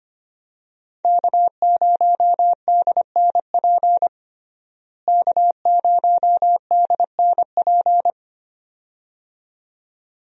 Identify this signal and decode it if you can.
Morse code, heard as a keyed tone, about 25 wpm, 700 Hz. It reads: K0DNP K0DNP
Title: USA Call Signs - 5 Characters - Repeated 2x - 25wpm